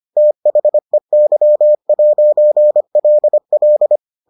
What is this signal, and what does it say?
Morse code, heard as a keyed tone, about 25 wpm, 600 Hz. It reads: THEY'LL